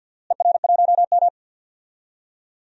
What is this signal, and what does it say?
Morse code, heard as a keyed tone, about 53 wpm, 700 Hz. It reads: ER1M
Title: DXCC-100 Call Signs - 4 Characters - Repeated 3x - 50wpm